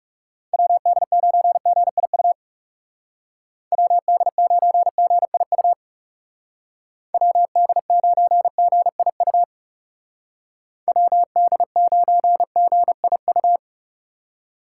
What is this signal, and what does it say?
Morse code, heard as a keyed tone, about 37 wpm, 700 Hz. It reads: WD9GIU WD9GIU WD9GIU WD9GIU
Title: Speed Racing - USA 2x3 Call Signs 30wpm